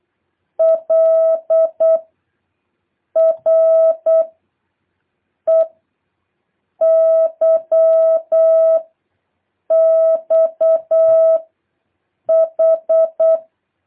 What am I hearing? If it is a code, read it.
Morse code, heard as a keyed tone, about 8 wpm, 650 Hz. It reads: L R E Y X H
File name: jh8cbh / 2023cw